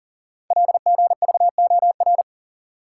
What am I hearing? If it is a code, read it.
Morse code, heard as a keyed tone, about 42 wpm, 690 Hz. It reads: LGVOR
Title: Code Groups - 5 Letters - 40wpm